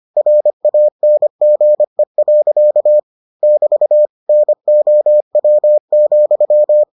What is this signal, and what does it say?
Morse code, heard as a keyed tone, about 25 wpm, 600 Hz. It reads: RANGE. <BT> NOW,